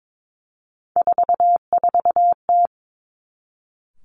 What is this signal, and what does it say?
Morse code, heard as a keyed tone, about 22 wpm, 700 Hz. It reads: 44T
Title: Three Challenging Letter-Number Combinations 22wpm